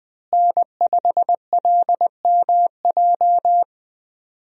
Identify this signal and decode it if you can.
Morse code, heard as a keyed tone, about 21 wpm, 700 Hz. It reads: N5LMJ
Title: USA Call Signs - 5 Characters - Repeated 3x - 20wpm